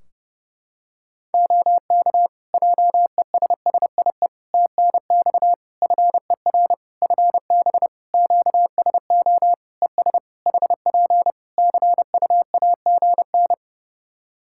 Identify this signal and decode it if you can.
Morse code, heard as a keyed tone, about 30 wpm, 700 Hz. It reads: OK JESSIE TNX FER FB QSO ES HP CUAGN
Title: Practice QSO - Exchange 4 - Ending - 30wpm